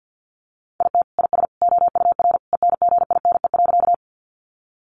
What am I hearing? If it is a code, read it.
Morse code, heard as a keyed tone, about 50 wpm, 700 Hz. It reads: IT IS OUR ENGINE.
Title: Sentences from Top 1500 Words 50wpm